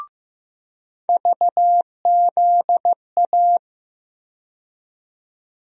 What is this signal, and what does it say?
Morse code, heard as a keyed tone, about 15 wpm, 700 Hz. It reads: VZA
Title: USA Call Sign Postfixes - 3 Letters 15wpm